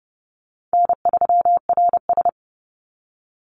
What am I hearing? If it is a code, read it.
Morse code, heard as a keyed tone, about 30 wpm, 700 Hz. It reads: N3RS
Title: Easy Call Signs - 30wpm